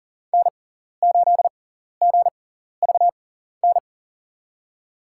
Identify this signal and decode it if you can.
Morse code, heard as a keyed tone, about 39 wpm, spaced as for 14 wpm, 700 Hz. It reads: N8GVN
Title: USA Call Signs - 5 Characters - 6x Character Spacing - 40wpm